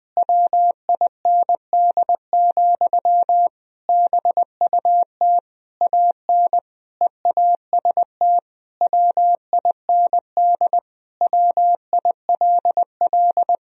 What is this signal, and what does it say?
Morse code, heard as a keyed tone, about 20 wpm, 700 Hz. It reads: WIND, BUT AN EAST WIND WILL
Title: Sentences from Top 1000 Words - Review Part 3 - 20wpm